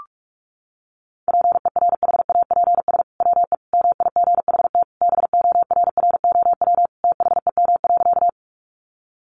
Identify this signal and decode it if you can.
Morse code, heard as a keyed tone, about 44 wpm, 700 Hz. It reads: PERHAPS WE MIGHT BORROW THEM.
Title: Sentences from Top 1500 Words 45wpm